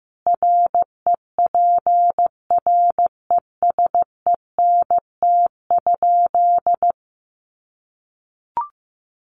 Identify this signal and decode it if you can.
Morse code, heard as a keyed tone, about 15 wpm, 700 Hz. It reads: REPRESENT?
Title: Sentences from Top 1200 Words 15wpm